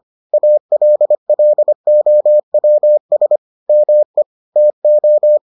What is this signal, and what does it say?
Morse code, heard as a keyed tone, about 25 wpm, 600 Hz. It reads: ALLOWS ME TO